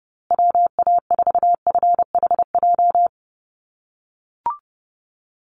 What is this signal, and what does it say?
Morse code, heard as a keyed tone, about 30 wpm, 700 Hz. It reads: WA4FHJ
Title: USA Call Signs Repeated 2x 30wpm